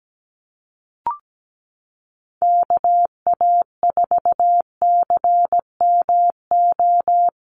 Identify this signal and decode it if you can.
Morse code, heard as a keyed tone, about 17 wpm, 700 Hz. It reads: KA4CMO
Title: USA Call Signs 17wpm